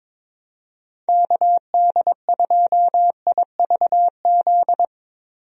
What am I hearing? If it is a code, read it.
Morse code, heard as a keyed tone, about 22 wpm, 700 Hz. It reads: KD2IVZ
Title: USA Call Signs - 6 Characters - Repeated 2x - 22wpm